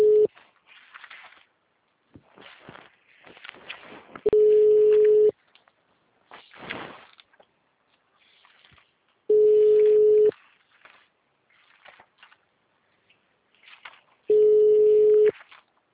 Ringback tone (Europe, 425 Hz).